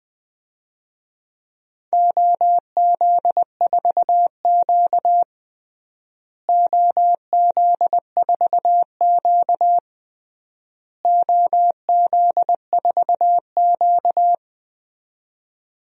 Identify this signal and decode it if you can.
Morse code, heard as a keyed tone, about 21 wpm, 700 Hz. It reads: OZ4Q OZ4Q OZ4Q
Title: DXCC-100 Call Signs - 4 Characters - Repeated 3x - 20wpm